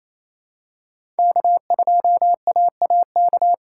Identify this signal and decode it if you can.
Morse code, heard as a keyed tone, about 28 wpm, 700 Hz. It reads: K2AAK
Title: USA Call Signs - 5 Characters - 2x Character Spacing - 28wpm